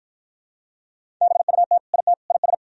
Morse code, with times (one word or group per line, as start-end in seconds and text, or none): 1.2-1.8 BUT
1.9-2.1 IT
2.3-2.5 IS